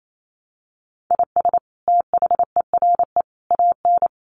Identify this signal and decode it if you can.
Morse code, heard as a keyed tone, about 28 wpm, 700 Hz. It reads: IS THERE AN
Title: Sentences from Top 1200 Words 28wpm